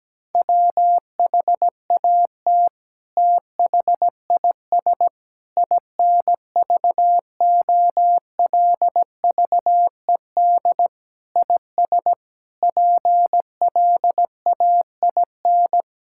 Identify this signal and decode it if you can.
Morse code, heard as a keyed tone, about 17 wpm, 700 Hz. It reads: WHAT THIS INVOLVED IS PLAIN
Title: Sentences from Top 1000 Words - Review Part 5 - 17wpm